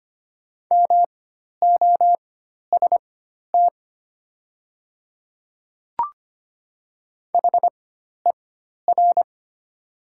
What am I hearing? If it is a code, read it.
Morse code, heard as a keyed tone, about 25 wpm, spaced as for 12 wpm, 700 Hz. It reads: MOST HER
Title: 4x Character Spacing - Top 100 Words - 25wpm